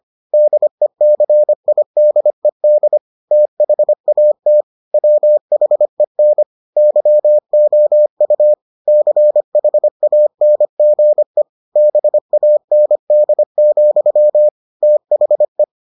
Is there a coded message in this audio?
Yes: Morse code from 0.3 to 15.6 s, heard as a keyed tone, about 25 wpm, 600 Hz. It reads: DECIDED THAT WHEN YOU CHANGE BAND, THE